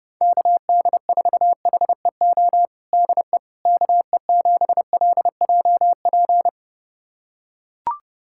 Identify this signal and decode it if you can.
Morse code, heard as a keyed tone, about 32 wpm, 700 Hz. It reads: KD4HEO DE KE7LJP
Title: USA Callsign DE Callsign 30wpm